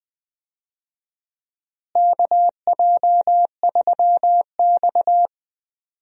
Morse code, heard as a keyed tone, about 20 wpm, 700 Hz. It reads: KJ3X
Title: USA Call Signs - 4 Characters - Repeated 2x - 20wpm